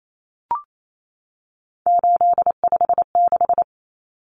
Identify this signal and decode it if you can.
Morse code, heard as a keyed tone, about 28 wpm, 700 Hz. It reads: 856
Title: Numbers - 3-Digit 28wpm